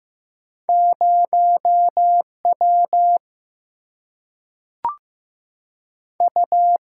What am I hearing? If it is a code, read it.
Morse code, heard as a keyed tone, about 15 wpm, 700 Hz. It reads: 0 W U
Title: Single-Letter Single-Number 15wpm